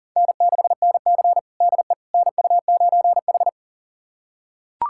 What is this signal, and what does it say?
Morse code, heard as a keyed tone, about 40 wpm, 700 Hz. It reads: N6NC DE NU9H
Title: USA Callsign DE Callsign - 4 Characters - 40wpm